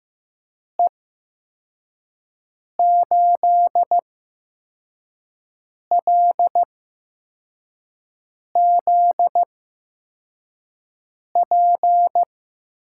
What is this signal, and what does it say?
Morse code, heard as a keyed tone, about 15 wpm, spaced as for 4 wpm, 700 Hz. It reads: E8LZP
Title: USA Call Signs - 6 Characters - 8x Character Spacing - 15wpm